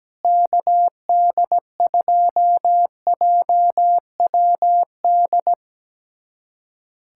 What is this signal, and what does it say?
Morse code, heard as a keyed tone, about 17 wpm, 700 Hz. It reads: KD2JWD
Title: USA Callsign DE Callsign 17wpm